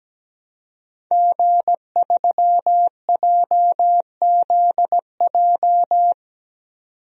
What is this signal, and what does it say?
Morse code, heard as a keyed tone, about 17 wpm, 700 Hz. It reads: G3JZJ
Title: DXCC-100 Call Signs - 5 Characters - 17wpm